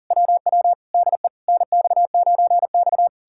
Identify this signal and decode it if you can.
Morse code, heard as a keyed tone, about 58 wpm, spaced as for 49 wpm, 700 Hz. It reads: WW DE NX9X
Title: Speed-Racing Common QSO Phrases Encore - 40wpm